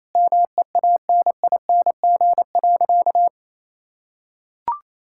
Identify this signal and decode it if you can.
Morse code, heard as a keyed tone, about 28 wpm, 700 Hz. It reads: MEANING.
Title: Sentences from Top 1000 Words - Review Part 6 - 28wpm